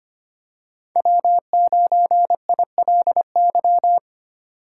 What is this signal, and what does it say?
Morse code, heard as a keyed tone, about 25 wpm, 700 Hz. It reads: W9ILY
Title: K1USN SST QSO Practice - 25wpm